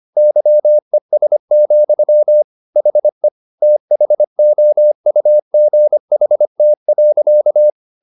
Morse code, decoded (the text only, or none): YES, HE THOUGHT.